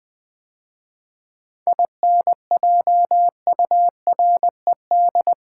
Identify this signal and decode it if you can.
Morse code, heard as a keyed tone, about 20 wpm, 700 Hz. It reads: INJURED
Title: 7 Letter Words - First 3 Letters Given - 20wpm